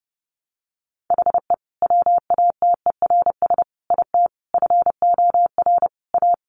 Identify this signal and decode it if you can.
Morse code, heard as a keyed tone, about 30 wpm, 700 Hz. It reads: HE WATERS IT FOR A
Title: Easy Phrases 30wpm